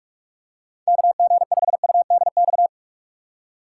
Morse code, heard as a keyed tone, about 46 wpm, 700 Hz. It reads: KG5UDX
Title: USA Call Signs - 4x Character Spacing - 45wpm